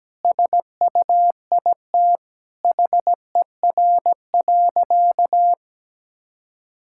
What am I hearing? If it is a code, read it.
Morse code, heard as a keyed tone, about 17 wpm, 700 Hz. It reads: SUIT HER.
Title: Sentences from Top 1600 Words 17wpm